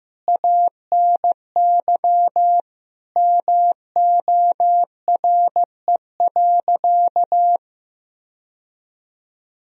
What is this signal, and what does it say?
Morse code, heard as a keyed tone, about 15 wpm, 700 Hz. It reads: ANY MORE.
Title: Sentences from Top 1000 Words - Review Part 1 - 15wpm